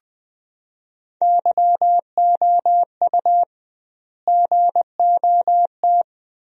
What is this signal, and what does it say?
Morse code, heard as a keyed tone, about 20 wpm, 700 Hz. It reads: YOU GOT
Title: 2x Word-Spacing - Sentences from Top 300 Words - 20wpm